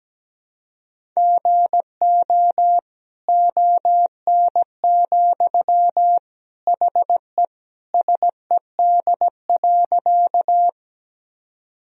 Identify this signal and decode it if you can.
Morse code, heard as a keyed tone, about 17 wpm, 700 Hz. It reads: GO ON, HE SED.
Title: Sentences from Top 100 Words and Abbreviations 17wpm